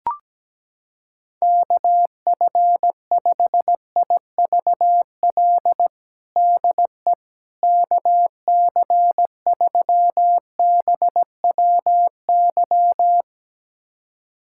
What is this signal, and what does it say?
Morse code, heard as a keyed tone, about 17 wpm, 700 Hz. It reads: KF5IVL DE KC3BWY
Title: USA Callsign DE Callsign - 6 Characters - 17wpm